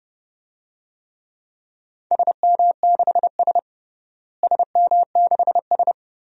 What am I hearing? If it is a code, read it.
Morse code, heard as a keyed tone, about 30 wpm, 700 Hz. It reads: SM6S SM6S
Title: DXCC-100 Call Signs - 4 Characters - Repeated 2x - 30wpm